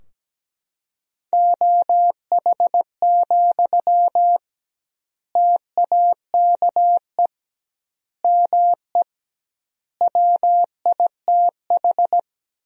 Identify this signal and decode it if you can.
Morse code, heard as a keyed tone, about 17 wpm, 700 Hz. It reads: OH, TAKE ME WITH
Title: 2x Word-Spacing - Sentences from Top 500 Words - 17wpm